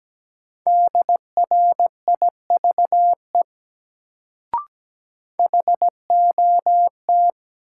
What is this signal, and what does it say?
Morse code, heard as a keyed tone, about 17 wpm, 700 Hz. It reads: DRIVE HOT
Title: Sentences from Top 500 Words - New Word Drill - 17wpm